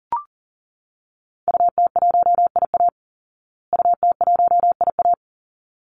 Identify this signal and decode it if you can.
Morse code, heard as a keyed tone, about 40 wpm, 700 Hz. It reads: UT1IA UT1IA
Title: Call Signs Repeated 3x - 40wpm